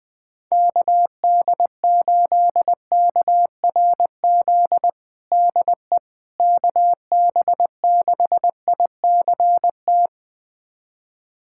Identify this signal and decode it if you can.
Morse code, heard as a keyed tone, about 21 wpm, 700 Hz. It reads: KD8KRZ DE KB6ICT
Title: USA Callsign DE Callsign 20wpm